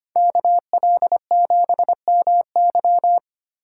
Morse code, heard as a keyed tone, about 25 wpm, 700 Hz. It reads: KL7MY
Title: USA Call Signs - 5 Characters - 3x Character Spacing - 25wpm